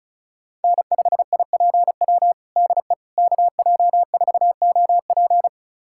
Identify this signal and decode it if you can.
Morse code, heard as a keyed tone, about 36 wpm, 700 Hz. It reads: N5IPW DE KJ4OP
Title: USA Callsign DE Callsign - 5 Characters - 35wpm